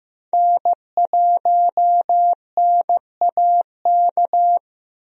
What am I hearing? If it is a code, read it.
Morse code, heard as a keyed tone, about 15 wpm, 700 Hz. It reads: N1NAK
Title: USA Call Signs - 5 Characters - Repeated 2x - 15wpm